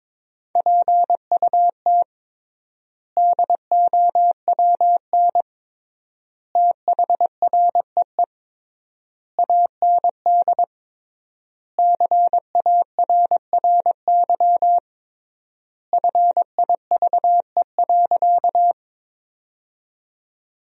Morse code, heard as a keyed tone, about 22 wpm, 700 Hz. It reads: PUT DOWN THREE AND CARRY FIVE.
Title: 3x Word-Spacing - Sentences from Top 400 Words - 22wpm